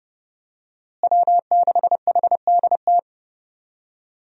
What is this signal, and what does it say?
Morse code, heard as a keyed tone, about 30 wpm, 700 Hz. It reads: W6HDT
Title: USA Call Signs - 5 Characters - Repeated 3x - 30wpm